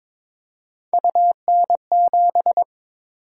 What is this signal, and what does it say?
Morse code, heard as a keyed tone, about 22 wpm, 700 Hz. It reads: UN7
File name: practice / 22wpm / Three Letter-Number Combinations - 22wpm